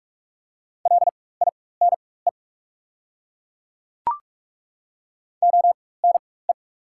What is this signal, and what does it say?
Morse code, heard as a keyed tone, about 42 wpm, spaced as for 21 wpm, 690 Hz. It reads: LINE ONE
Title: 4x Character Spacing - Top 200 Words - 45wpm